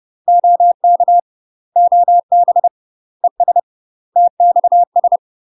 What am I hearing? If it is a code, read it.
Morse code, heard as a keyed tone, about 30 wpm, 700 Hz. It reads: OK OB ES TXS
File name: UK MASTER QSOs TRAINING / 30WPM / UK MASTER QSOs 0048